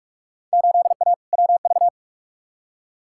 Morse code, heard as a keyed tone, about 46 wpm, 700 Hz. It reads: 8A WV
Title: Field Day Exchanges - 45wpm